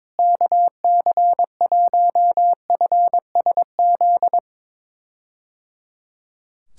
Morse code, heard as a keyed tone, about 21 wpm, 700 Hz. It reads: KC1FSZ
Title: K1USN SST QSO Practice - 22wpm